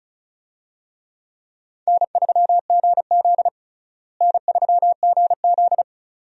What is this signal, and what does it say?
Morse code, heard as a keyed tone, about 34 wpm, 700 Hz. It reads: N3GZ N3GZ
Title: USA Call Signs - 4 Characters - Repeated 2x - 35wpm